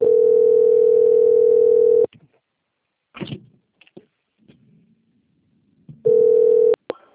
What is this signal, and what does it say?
Ringback tone (North America), 440 + 480 Hz.